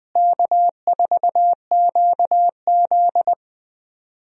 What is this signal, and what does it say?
Morse code, heard as a keyed tone, about 20 wpm, 700 Hz. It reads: K4QZ
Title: USA Call Signs - 4 Characters - Repeated 3x - 20wpm